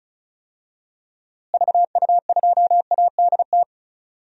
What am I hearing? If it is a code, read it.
Morse code, heard as a keyed tone, about 36 wpm, 700 Hz. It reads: VU2ADT
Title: DXCC-100 Call Signs - 6 Characters - Repeated 2x - 35wpm